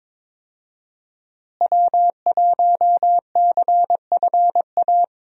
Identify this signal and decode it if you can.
Morse code, heard as a keyed tone, about 21 wpm, 700 Hz. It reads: W1CFA
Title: USA Call Signs - 5 Characters - Repeated 3x - 22wpm